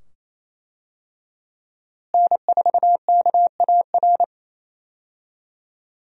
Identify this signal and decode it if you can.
Morse code, heard as a keyed tone, about 28 wpm, 700 Hz. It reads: N4KAR
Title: USA Call Signs - 5 Characters - Repeated 2x - 28wpm